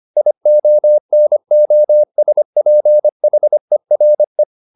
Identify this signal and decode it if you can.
Morse code, heard as a keyed tone, about 25 wpm, 600 Hz. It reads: IONOSPHERE
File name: SS311 25